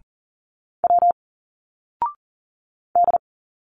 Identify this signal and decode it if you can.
Morse code, heard as a keyed tone, about 40 wpm, 700 Hz. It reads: W D
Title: Single Letters 40wpm